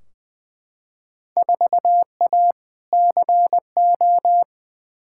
Morse code, heard as a keyed tone, about 20 wpm, 700 Hz. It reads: 4A CO
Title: Field Day Exchanges - 20wpm